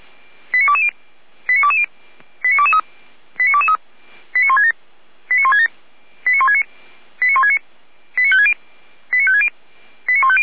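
Five-tone selective call (ZVEI), heard as quick runs of five tones, tones of about 70 ms. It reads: 89220 89220 89223 89223 89167 89167 89169 89169 89560 89560 89150
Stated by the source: ZVEI five-tone selective call received via SDR.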